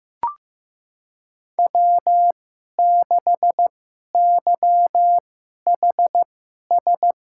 Morse code, heard as a keyed tone, about 15 wpm, spaced as for 11 wpm, 700 Hz. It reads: W6YHS
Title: USA Call Signs - 5 Characters - 2x Character Spacing - 15wpm